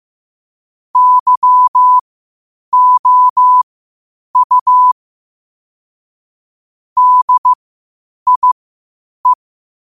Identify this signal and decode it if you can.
Morse code, heard as a keyed tone, about 15 wpm, spaced as for 8 wpm, 1000 Hz. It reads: YOU DIE
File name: qotdfiles / 8wpm